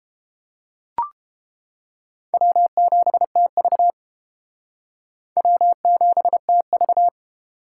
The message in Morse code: W7TV W7TV

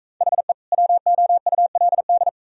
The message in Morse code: HE WOULD

